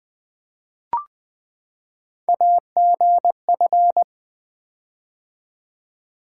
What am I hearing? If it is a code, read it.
Morse code, heard as a keyed tone, about 20 wpm, 700 Hz. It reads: AGF
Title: Three Letter-Number Combinations - 20wpm